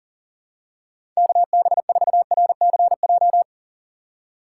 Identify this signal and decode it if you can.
Morse code, heard as a keyed tone, about 37 wpm, 700 Hz. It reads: KB4RCJ
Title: USA Call Signs - 6 Characters - 2x Character Spacing - 40wpm